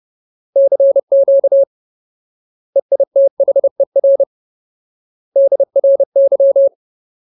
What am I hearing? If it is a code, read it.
Morse code, heard as a keyed tone, about 30 wpm, 550 Hz. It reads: CQ EITHER DRY
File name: Ultimate Word List / 30WPM Ultimate